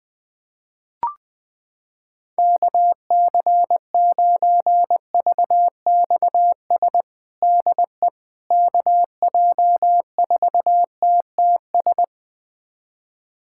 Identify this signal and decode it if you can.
Morse code, heard as a keyed tone, about 20 wpm, 700 Hz. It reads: KC9VXS DE KJ4TTS
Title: USA Callsign DE Callsign - 6 Characters - 20wpm